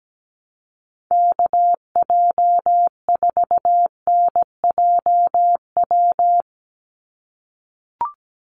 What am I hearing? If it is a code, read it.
Morse code, heard as a keyed tone, about 17 wpm, 700 Hz. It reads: KJ4NJW
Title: USA Call Signs 17wpm